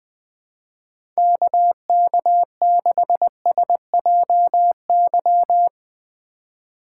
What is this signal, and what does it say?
Morse code, heard as a keyed tone, about 20 wpm, 700 Hz. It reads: KK6SJY
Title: USA Call Signs - 6 Characters - 4x Character Spacing - 20wpm